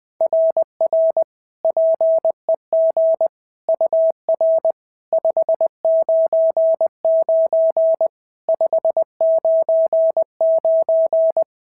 Morse code, heard as a keyed tone, about 20 wpm, 650 Hz. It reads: RR PEG UR 599 599
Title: SOTA QSO Practice 20wpm